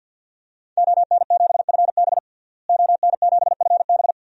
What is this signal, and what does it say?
Morse code, heard as a keyed tone, about 50 wpm, 700 Hz. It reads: ON7FB ON7FB
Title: DXCC-100 Call Signs - 5 Characters - Repeated 3x - 50wpm